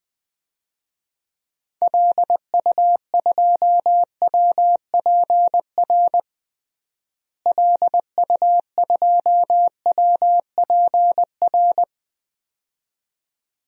Morse code, heard as a keyed tone, about 20 wpm, 700 Hz. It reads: LU2WPR LU2WPR
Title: DXCC-100 Call Signs - 6 Characters - Repeated 2x - 20wpm